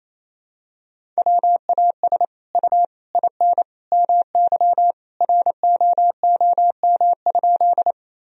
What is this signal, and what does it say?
Morse code, heard as a keyed tone, about 28 wpm, 700 Hz. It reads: WAS U IN MY ROOM?
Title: Sentences from Top 300 Words and Abbreviations 28wpm